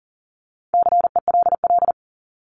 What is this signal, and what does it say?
Morse code, heard as a keyed tone, about 40 wpm, 700 Hz. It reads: CELL
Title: ES:S Suffix - Top 100 Words with Base Words 40wpm